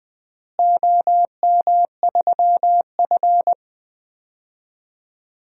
Morse code, heard as a keyed tone, about 20 wpm, 700 Hz. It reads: OM3F
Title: DXCC-100 Call Signs - 4 Characters - Repeated 2x - 20wpm